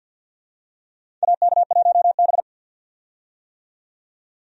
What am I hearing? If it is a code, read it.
Morse code, heard as a keyed tone, about 47 wpm, 700 Hz. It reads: AK1B
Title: USA Call Signs - 4 Characters - 50wpm